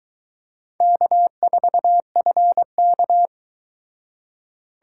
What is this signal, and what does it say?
Morse code, heard as a keyed tone, about 21 wpm, 700 Hz. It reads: K4FK